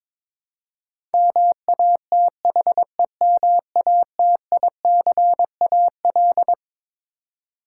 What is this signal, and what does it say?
Morse code, heard as a keyed tone, about 22 wpm, 700 Hz. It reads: MATHEMATICAL
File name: Twelve Letter Words - 22wpm